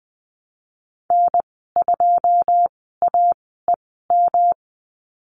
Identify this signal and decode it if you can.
Morse code, heard as a keyed tone, about 20 wpm, spaced as for 15 wpm, 700 Hz. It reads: N2AEM
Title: 161 Lesson 26g - Letter g - Call Signs 20wpm